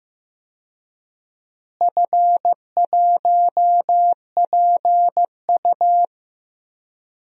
Morse code, heard as a keyed tone, about 15 wpm, 700 Hz. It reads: F1PU
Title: DXCC-100 Call Signs - 4 Characters - Repeated 3x - 15wpm